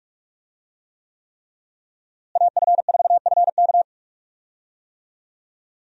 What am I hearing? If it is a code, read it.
Morse code, heard as a keyed tone, about 41 wpm, 700 Hz. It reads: AF4FK
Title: USA Call Signs - 5 Characters - Repeated 3x - 45wpm